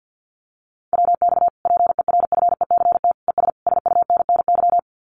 Morse code, heard as a keyed tone, about 50 wpm, 700 Hz. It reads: WX PERFECT ES SUNNY
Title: Common QSO Phrases Encore 50wpm